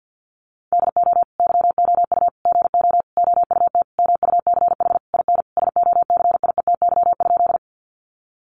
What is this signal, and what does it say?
Morse code, heard as a keyed tone, about 50 wpm, 700 Hz. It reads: DO YOU GO OUT MUCH IN SOCIETY?
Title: Sentences from Top 700 Words 50wpm